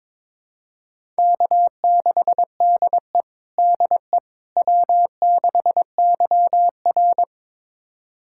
Morse code, heard as a keyed tone, about 22 wpm, 700 Hz. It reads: K6DE DE W6YR
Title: USA Callsign DE Callsign - 4 Characters - 22wpm